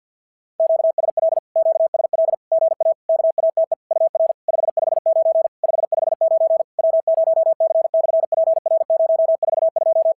Morse code, heard as a keyed tone, about 51 wpm, 650 Hz. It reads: QSL QSL GA KATE UR 559 559 W0C/PR042
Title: SOTA QSO Practice 50wpm